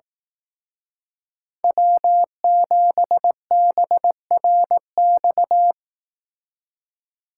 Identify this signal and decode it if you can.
Morse code, heard as a keyed tone, about 18 wpm, 700 Hz. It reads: W7BRX
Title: Speed Racing - USA 1x3 Call Signs 15wpm